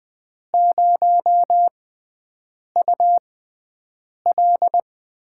Morse code, heard as a keyed tone, about 20 wpm, spaced as for 7 wpm, 700 Hz. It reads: 0UL
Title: USA Call Signs - 6 Characters - 6x Character Spacing - 20wpm